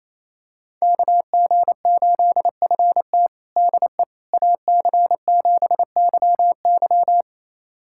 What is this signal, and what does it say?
Morse code, heard as a keyed tone, about 28 wpm, 700 Hz. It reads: KG8FT DE AC7YY
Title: USA Callsign DE Callsign - 5 Characters - 28wpm